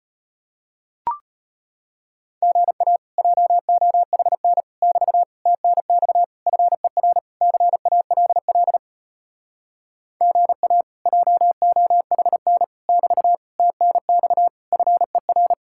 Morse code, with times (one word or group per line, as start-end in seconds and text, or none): 2.4-3.0 GA
3.2-4.6 JOHN
4.8-5.2 <BT>
5.5-6.2 TNX
6.5-7.2 FER
7.4-8.8 CALL
10.2-10.8 GA
11.1-12.6 JOHN
12.9-13.4 <BT>
13.6-14.5 TNX
14.7-15.5 FER